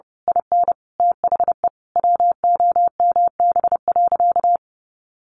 Morse code, heard as a keyed tone, about 30 wpm, 700 Hz. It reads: IN THE WOMB.
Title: A Sound Of Thunder 30wpm